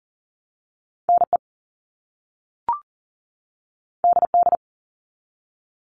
Morse code, heard as a keyed tone, about 38 wpm, 700 Hz. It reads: NE DD
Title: All Cut Numbers - 2-Digit 40wpm